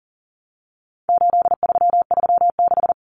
Morse code, heard as a keyed tone, about 40 wpm, 700 Hz. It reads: 8336
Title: Common Cut Numbers - 4-Digit 40wpm